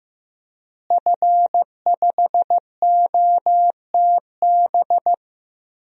Morse code, heard as a keyed tone, about 15 wpm, 700 Hz. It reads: F5OTB
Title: DXCC-100 Call Signs - 5 Characters - Repeated 3x - 15wpm